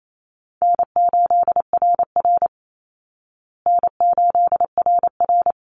Morse code, heard as a keyed tone, about 28 wpm, 700 Hz. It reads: N8RR N8RR
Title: Easy Call Signs Repeated 2x - 28wpm